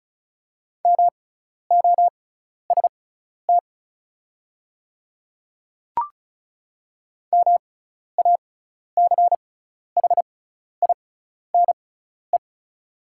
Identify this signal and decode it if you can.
Morse code, heard as a keyed tone, about 34 wpm, spaced as for 12 wpm, 700 Hz. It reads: MOST MACHINE
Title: 6x Character Spacing - Top 500 Words - 35wpm